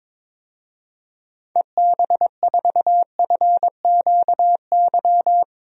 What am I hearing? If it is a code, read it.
Morse code, heard as a keyed tone, about 22 wpm, 700 Hz. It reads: EB4FQY
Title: DXCC-100 Call Signs - 6 Characters - Repeated 3x - 22wpm